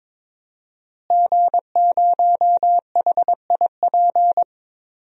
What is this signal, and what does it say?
Morse code, heard as a keyed tone, about 22 wpm, 700 Hz. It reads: G0HIP
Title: DXCC-100 Call Signs - 5 Characters - Repeated 3x - 22wpm